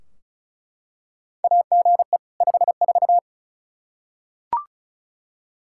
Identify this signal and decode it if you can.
Morse code, heard as a keyed tone, about 36 wpm, 700 Hz. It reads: AGE 54
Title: Practice QSO - Exchange 3 - More Info - 35wpm